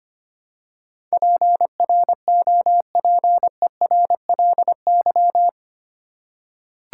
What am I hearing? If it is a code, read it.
Morse code, heard as a keyed tone, about 25 wpm, 700 Hz. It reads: PROPERLY
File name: practice / 25wpm / 8 Letter Words - First 3 Letters Given - 25wpm